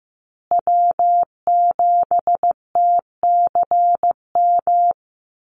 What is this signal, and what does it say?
Morse code, heard as a keyed tone, about 15 wpm, 700 Hz. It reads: W7TCM
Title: USA Call Signs Repeated 2x 15wpm